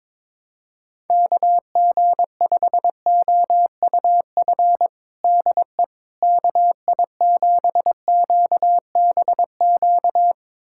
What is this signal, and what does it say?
Morse code, heard as a keyed tone, about 22 wpm, 700 Hz. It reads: KG5OUF DE KI7QBQ
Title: USA Callsign DE Callsign 22wpm